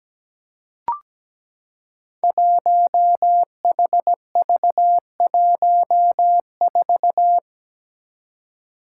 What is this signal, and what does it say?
Morse code, heard as a keyed tone, about 17 wpm, 700 Hz. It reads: 1HV14
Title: Five Challenging Letter-Number Combinations 17wpm